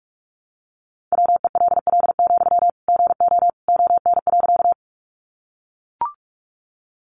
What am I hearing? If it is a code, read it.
Morse code, heard as a keyed tone, about 44 wpm, 700 Hz. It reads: WELL, GO ON.
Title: Sentences from Top 200 Words 45wpm